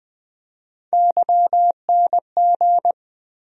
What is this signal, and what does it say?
Morse code, heard as a keyed tone, about 20 wpm, 700 Hz. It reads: YNG
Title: USA Call Sign Postfixes - 3 Letters 20wpm